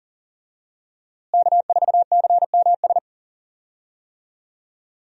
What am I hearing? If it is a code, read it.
Morse code, heard as a keyed tone, about 40 wpm, 700 Hz. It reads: K4CMS
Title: USA Call Signs - 5 Characters - Repeated 2x - 40wpm